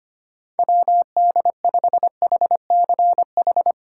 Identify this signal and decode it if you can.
Morse code, heard as a keyed tone, about 28 wpm, 700 Hz. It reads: WD5HCH
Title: Speed Racing - USA 2x3 Call Signs 25wpm